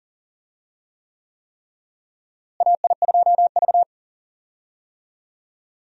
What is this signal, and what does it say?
Morse code, heard as a keyed tone, about 39 wpm, 700 Hz. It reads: AI2V
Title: USA Call Signs - 4 Characters - Repeated 3x - 40wpm